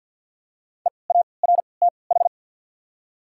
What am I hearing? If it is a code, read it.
Morse code, heard as a keyed tone, about 53 wpm, spaced as for 29 wpm, 690 Hz. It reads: EARTH
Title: 3x Character Spacing - Top 200 Words - 50wpm